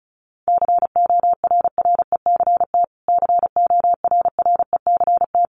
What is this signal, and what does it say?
Morse code, heard as a keyed tone, about 34 wpm, 700 Hz. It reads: CORRECT CORRECT
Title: Top 500 Words Repeated 35wpm